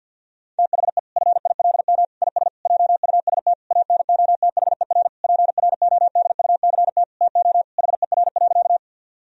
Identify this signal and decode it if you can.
Morse code, heard as a keyed tone, about 50 wpm, 700 Hz. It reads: THE FILM IS JUST ANOTHER PRODUCT TO HER.
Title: Sentences from Top 1000 Words - Review Part 3 - 50wpm